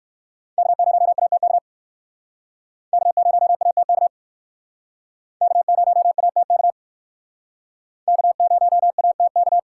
Morse code, heard as a keyed tone, about 56 wpm, 700 Hz. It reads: K0ATK K0ATK K0ATK K0ATK
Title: Speed Racing - USA 1x3 Call Signs 45wpm